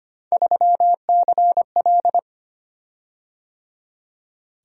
Morse code, heard as a keyed tone, about 25 wpm, 700 Hz. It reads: 3CL
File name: Three Letter-Number Combinations - 25wpm